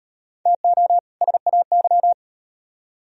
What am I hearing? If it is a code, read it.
Morse code, heard as a keyed tone, about 35 wpm, 700 Hz. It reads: TO SAY